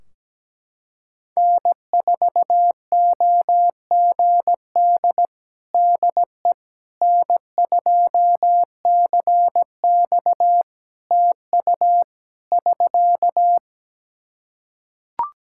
Morse code, heard as a keyed tone, about 17 wpm, 700 Hz. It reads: N4OGD DE N2CX TU <SK>
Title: Practice QSO - Exchange 4 - Ending - 17wpm